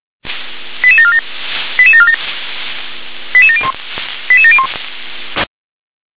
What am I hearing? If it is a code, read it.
Five-tone selective call (ZVEI), heard as quick runs of five tones, tones of about 70 ms. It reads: 88747 88747 88701 88701